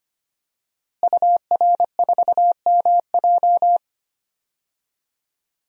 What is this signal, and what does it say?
Morse code, heard as a keyed tone, about 25 wpm, 700 Hz. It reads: UR4MJ
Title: DXCC-100 Call Signs - 5 Characters - Repeated 2x - 25wpm